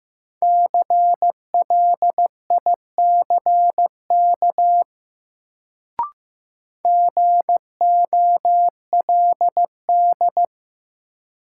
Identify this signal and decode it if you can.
Morse code, heard as a keyed tone, about 15 wpm, 700 Hz. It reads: CLICK GOLD
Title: Sentences from Top 1100 Words - New Word Drill Alt - 15wpm